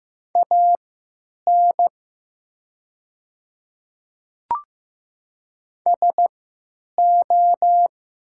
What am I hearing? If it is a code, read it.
Morse code, heard as a keyed tone, about 15 wpm, spaced as for 9 wpm, 700 Hz. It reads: AN SO
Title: 3x Character Spacing - Top 200 Words - 15wpm